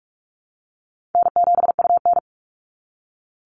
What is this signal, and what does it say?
Morse code, heard as a keyed tone, about 46 wpm, 700 Hz. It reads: N7UN
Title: NA SOTA Activator Callsigns 45wpm